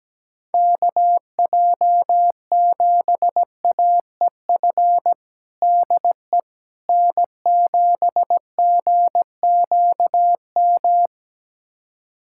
Morse code, heard as a keyed tone, about 17 wpm, 700 Hz. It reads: KJ7AEF DE N7GQM
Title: USA Callsign DE Callsign 17wpm